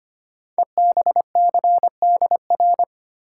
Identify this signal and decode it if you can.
Morse code, heard as a keyed tone, about 25 wpm, 700 Hz. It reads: EBCDR
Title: Code Groups - 5 Letters - 25wpm